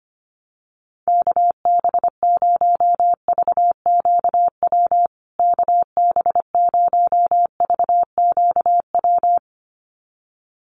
Morse code, heard as a keyed tone, about 25 wpm, 700 Hz. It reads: KB0VQW KB0VQW
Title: USA Call Signs Repeated 2x 25wpm